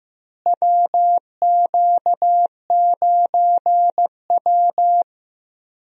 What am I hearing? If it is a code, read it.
Morse code, heard as a keyed tone, about 16 wpm, 700 Hz. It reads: WQ9W
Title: USA Call Signs - 4 Characters - Repeated 2x - 15wpm